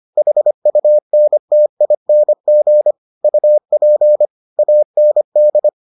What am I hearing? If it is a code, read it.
Morse code, heard as a keyed tone, about 25 wpm, 600 Hz. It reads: HUNTING UP AND